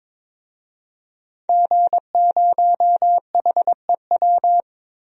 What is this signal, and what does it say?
Morse code, heard as a keyed tone, about 22 wpm, 700 Hz. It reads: G0HEW
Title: DXCC-100 Call Signs - 5 Characters - Repeated 3x - 22wpm